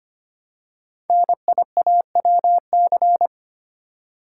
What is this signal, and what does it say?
Morse code, heard as a keyed tone, about 25 wpm, 700 Hz. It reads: NIAWC
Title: Code Groups - 5 Letters - 25wpm